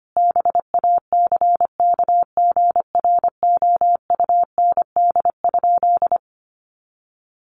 Morse code, heard as a keyed tone, about 25 wpm, 700 Hz. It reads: BACKGROUND?
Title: Sentences from Top 1300 Words 25wpm